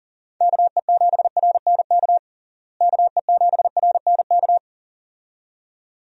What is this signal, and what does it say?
Morse code, heard as a keyed tone, about 40 wpm, 700 Hz. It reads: KE7RNK KE7RNK
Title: USA Call Signs - 6 Characters - Repeated 2x - 40wpm